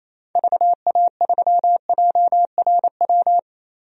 Morse code, heard as a keyed tone, about 28 wpm, 700 Hz. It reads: VA3JRW
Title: DXCC-100 Call Signs - 6 Characters - 28wpm